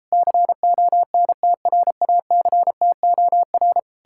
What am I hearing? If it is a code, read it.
Morse code, heard as a keyed tone, about 37 wpm, 700 Hz. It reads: CONTRACTOR